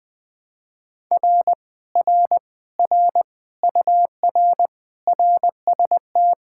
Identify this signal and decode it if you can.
Morse code, heard as a keyed tone, about 20 wpm, 700 Hz. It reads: R R R UR RST
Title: Speed-Racing Common QSO Phrases Encore - 17wpm